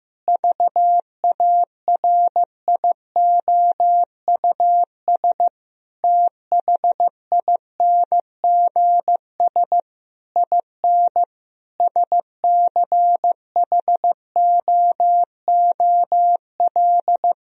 Morse code, heard as a keyed tone, about 15 wpm, 700 Hz. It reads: VARIOUS THINGS IN SCHOOL
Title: Sentences from Top 1000 Words - Review Part 4 - 15wpm